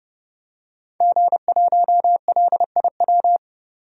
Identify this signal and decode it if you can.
Morse code, heard as a keyed tone, about 32 wpm, 700 Hz. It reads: G1LIW
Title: DXCC-100 Call Signs - 5 Characters - 30wpm